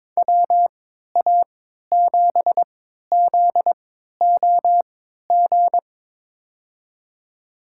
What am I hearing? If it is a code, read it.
Morse code, heard as a keyed tone, about 22 wpm, spaced as for 12 wpm, 700 Hz. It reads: WA7ZOG
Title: USA Call Signs - 6 Characters - 3x Character Spacing - 22wpm